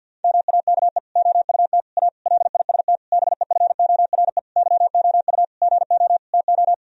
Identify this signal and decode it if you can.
Morse code, heard as a keyed tone, about 47 wpm, 700 Hz. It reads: MAKE OUT A LIST BEFORE YOU GO TO
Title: Sentences from Top 1000 Words - Review Part 2 - 50wpm